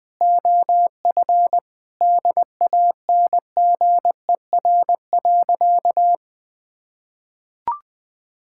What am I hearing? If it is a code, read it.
Morse code, heard as a keyed tone, about 20 wpm, 690 Hz. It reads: OF DANGER.
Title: Sentences with 4-Word Phrases - Spoken Beginning - 20wpm